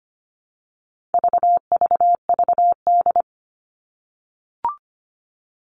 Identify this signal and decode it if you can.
Morse code, heard as a keyed tone, about 25 wpm, 700 Hz. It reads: VVVD
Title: All Cut Numbers - 4-Digit 25wpm